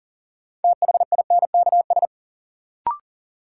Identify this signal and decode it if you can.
Morse code, heard as a keyed tone, about 40 wpm, 700 Hz. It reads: THINKS
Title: Sentences from Top 1000 Words - Review Part 6 - New Word Drill - 40wpm